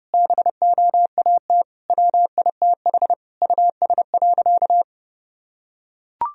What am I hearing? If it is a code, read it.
Morse code, heard as a keyed tone, about 30 wpm, 700 Hz. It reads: BOAT WITH US.
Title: Sentences from Top 1000 Words - Review Part 3 - 30wpm